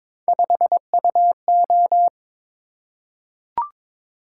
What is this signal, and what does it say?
Morse code, heard as a keyed tone, about 22 wpm, 700 Hz. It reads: 5UO
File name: practice / 22wpm / Three Letter-Number Combinations - 22wpm